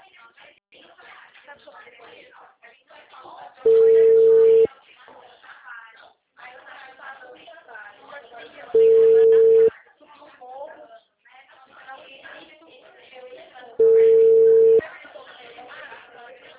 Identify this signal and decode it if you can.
Ringback tone (China, 450 Hz), with speech.